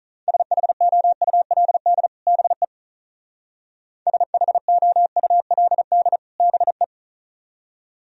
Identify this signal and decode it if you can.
Morse code, heard as a keyed tone, about 42 wpm, 700 Hz. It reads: SHOULD BE SHOULD BE
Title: Speed-Racing - Sets of 2 Words 35wpm